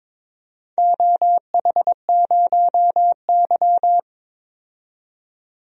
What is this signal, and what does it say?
Morse code, heard as a keyed tone, about 21 wpm, 700 Hz. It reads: OH0Y
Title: DXCC-100 Call Signs - 4 Characters - 22wpm